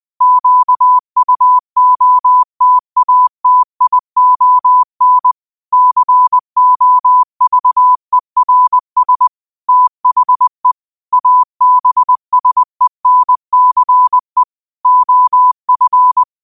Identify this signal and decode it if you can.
Morse code, heard as a keyed tone, about 20 wpm, 1000 Hz. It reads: QUOTATION COVERS THE ABSENCE OF
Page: New quotes every day in morse code at 20 Words per minute.